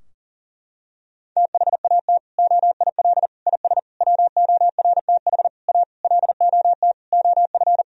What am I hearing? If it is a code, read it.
Morse code, heard as a keyed tone, about 40 wpm, 700 Hz. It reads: THAT OIL IS WORTH A LOT OF
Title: Sentences from Top 1000 Words - Review Part 4 - 40wpm